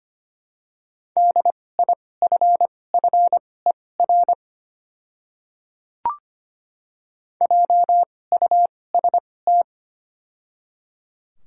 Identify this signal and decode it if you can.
Morse code, heard as a keyed tone, about 25 wpm, spaced as for 18 wpm, 700 Hz. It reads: DIFFER JUST
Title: 2x Character Spacing - Top 200 Words - 25wpm